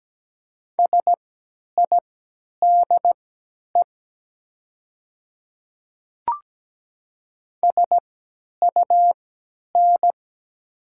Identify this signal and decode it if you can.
Morse code, heard as a keyed tone, about 17 wpm, spaced as for 10 wpm, 690 Hz. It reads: SIDE SUN
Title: 3x Character Spacing - Top 300 Words - 17wpm